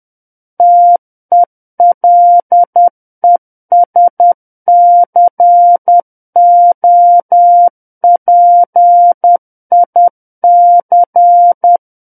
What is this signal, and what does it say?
Morse code, heard as a keyed tone, about 10 wpm, 700 Hz. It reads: TELESCOPIC
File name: Book One / 01-Chapter 1 - 10 WPM